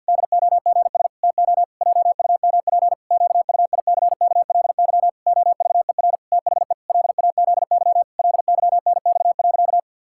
Morse code, heard as a keyed tone, about 51 wpm, 700 Hz. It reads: DOGS TO JUMP QUICKLY OVER THE LAZY LYNX.
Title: Pangram Challenge v2 - 50wpm